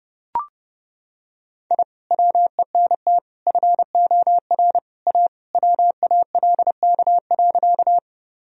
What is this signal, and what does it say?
Morse code, heard as a keyed tone, about 32 wpm, 700 Hz. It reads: I WENT FOR A WALK.
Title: Sentences from Top 1000 Words - Review Part 1 - 30wpm